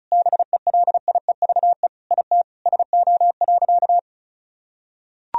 Morse code, heard as a keyed tone, about 34 wpm, 700 Hz. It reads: BELIEVE IT SO.
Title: Sentences from Top 1000 Words - Review Part 4 - 35wpm